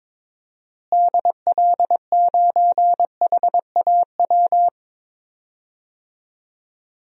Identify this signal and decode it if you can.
Morse code, heard as a keyed tone, about 22 wpm, 700 Hz. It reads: DL9HAW
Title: DXCC-100 Call Signs - 6 Characters - Repeated 3x - 22wpm